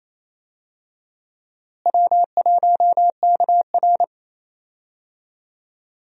Morse code, heard as a keyed tone, about 28 wpm, 700 Hz. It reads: W1KR
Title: USA Call Signs - 4 Characters - Repeated 3x - 28wpm